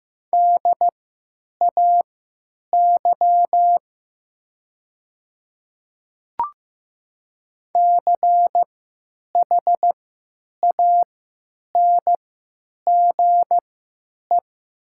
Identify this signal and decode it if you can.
Morse code, heard as a keyed tone, about 15 wpm, spaced as for 9 wpm, 700 Hz. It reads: DAY CHANGE
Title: 3x Character Spacing - Top 200 Words - 15wpm